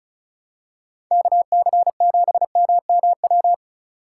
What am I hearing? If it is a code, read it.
Morse code, heard as a keyed tone, about 36 wpm, 700 Hz. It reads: KC7MMW
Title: USA Call Signs - 6 Characters - 2x Character Spacing - 35wpm